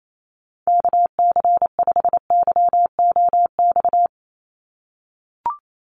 Morse code, heard as a keyed tone, about 28 wpm, 700 Hz. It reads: KC5YOX
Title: USA Call Signs 28wpm